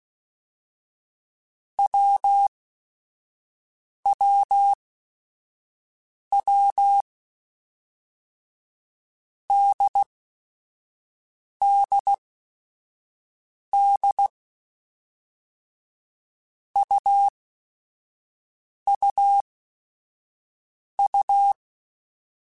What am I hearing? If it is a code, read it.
Morse code, heard as a keyed tone, about 16 wpm, spaced as for 5 wpm, 800 Hz. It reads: WWW DDD UUU
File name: Basic Sets / Basic Letter Groups Collective7 / AWUNDG@5